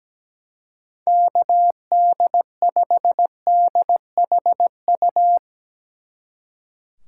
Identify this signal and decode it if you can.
Morse code, heard as a keyed tone, about 18 wpm, 700 Hz. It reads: KD5DHU
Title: Speed Racing - USA Call Signs 17wpm